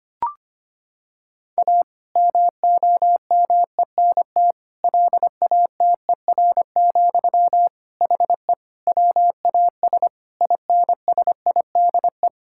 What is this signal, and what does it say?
Morse code, heard as a keyed tone, about 25 wpm, 700 Hz. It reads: A MOMENT LATER, HE WAS INSIDE
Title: Sentences from Top 1000 Words - Review Part 6 - 25wpm